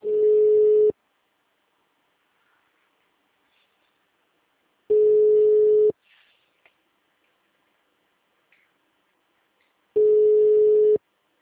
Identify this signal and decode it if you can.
Ringback tone (Europe, 425 Hz).